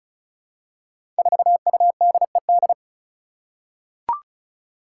Morse code, heard as a keyed tone, about 36 wpm, 700 Hz. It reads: 4UDED
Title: Five Challenging Letter-Number Combinations 35wpm